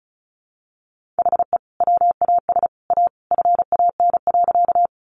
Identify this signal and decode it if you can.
Morse code, heard as a keyed tone, about 34 wpm, 700 Hz. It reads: HE WAS A FAN.
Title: Sentences from Top 1300 Words 35wpm